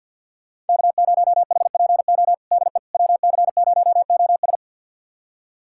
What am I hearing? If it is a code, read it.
Morse code, heard as a keyed tone, about 52 wpm, 700 Hz. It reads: K0HPO DE WC0OS
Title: USA Callsign DE Callsign - 5 Characters - 50wpm